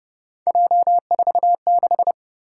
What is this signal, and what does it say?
Morse code, heard as a keyed tone, about 30 wpm, 700 Hz. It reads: J46